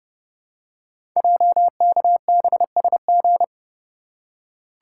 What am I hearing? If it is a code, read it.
Morse code, heard as a keyed tone, about 28 wpm, 700 Hz. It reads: JKBSG
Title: Code Groups - 5 Letters - 30wpm